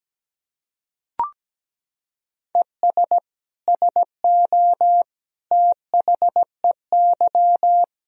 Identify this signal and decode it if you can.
Morse code, heard as a keyed tone, about 17 wpm, 700 Hz. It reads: ES SO THEY
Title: Sentences from Top 400 Words and Abbreviations 17wpm